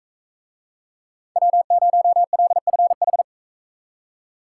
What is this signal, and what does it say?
Morse code, heard as a keyed tone, about 39 wpm, 700 Hz. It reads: W0LFH